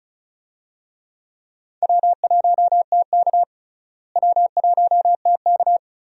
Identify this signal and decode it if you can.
Morse code, heard as a keyed tone, about 36 wpm, 700 Hz. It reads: W1TK W1TK
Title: USA Call Signs - 4 Characters - Repeated 2x - 35wpm